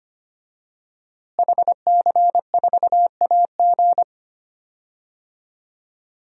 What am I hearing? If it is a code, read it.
Morse code, heard as a keyed tone, about 25 wpm, 700 Hz. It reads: HC4AG
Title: DXCC-100 Call Signs - 5 Characters - Repeated 2x - 25wpm